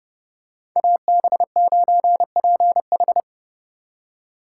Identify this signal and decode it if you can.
Morse code, heard as a keyed tone, about 28 wpm, 700 Hz. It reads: AB9PH